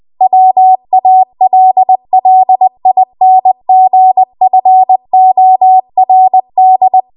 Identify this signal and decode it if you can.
Morse code, heard as a keyed tone, about 20 wpm, 750 Hz. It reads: WALLINGFORD
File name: Morse / 20WPM